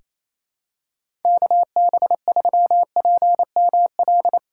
Morse code, heard as a keyed tone, about 28 wpm, 700 Hz. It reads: KB3PML
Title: USA Callsign DE Callsign - 6 Characters - 28wpm